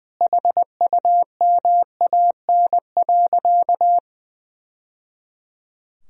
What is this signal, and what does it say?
Morse code, heard as a keyed tone, about 20 wpm, 700 Hz. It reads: HUMAN.
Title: Sentences from Top 1000 Words - Review Part 3 - 20wpm